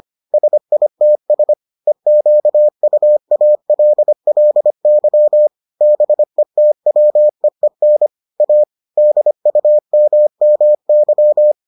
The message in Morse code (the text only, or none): SITS EQUALLY BETWEEN A DUMMY